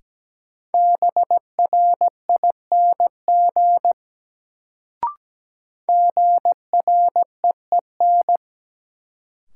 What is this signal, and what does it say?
Morse code, heard as a keyed tone, about 17 wpm, 700 Hz. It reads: BRING GREEN
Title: Sentences from Top 500 Words - New Word Drill - 17wpm